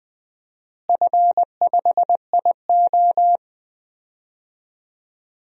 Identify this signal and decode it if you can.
Morse code, heard as a keyed tone, about 20 wpm, 700 Hz. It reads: F5IO
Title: DXCC-100 Call Signs - 4 Characters - Repeated 3x - 20wpm